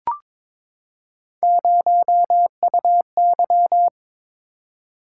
Morse code, heard as a keyed tone, about 22 wpm, 700 Hz. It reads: 0UY
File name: practice / 22wpm / Three Letter-Number Combinations - 22wpm